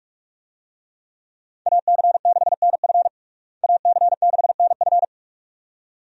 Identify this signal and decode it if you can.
Morse code, heard as a keyed tone, about 46 wpm, 700 Hz. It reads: AC6NF AC6NF
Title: USA Call Signs - 5 Characters - Repeated 2x - 45wpm